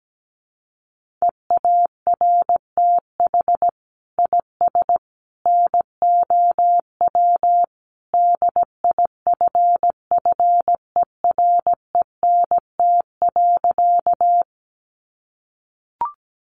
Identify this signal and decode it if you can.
Morse code, heard as a keyed tone, about 17 wpm, 700 Hz. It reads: EARTH IS NOW DIFFERENT.
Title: Sentences from Top 600 Words 17wpm